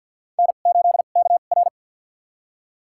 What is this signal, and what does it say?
Morse code, heard as a keyed tone, about 53 wpm, spaced as for 37 wpm, 700 Hz. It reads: N8KR
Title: USA Call Signs - 4 Characters - 2x Character Spacing - 50wpm